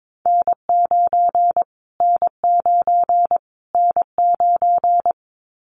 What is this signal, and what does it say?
Morse code, heard as a keyed tone, about 22 wpm, 700 Hz. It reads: N9 N9 N9
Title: Callsign Prefixes for Top 25 DXCC - Repeated 3x - 22wpm